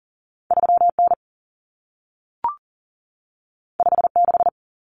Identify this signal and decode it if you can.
Morse code, heard as a keyed tone, about 40 wpm, 700 Hz. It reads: 3N 56
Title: Common Cut Numbers - 2-Digit 40wpm